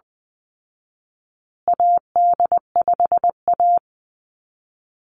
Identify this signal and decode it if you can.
Morse code, heard as a keyed tone, about 20 wpm, 700 Hz. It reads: AD5A
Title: Call Signs Repeated 3x - 20wpm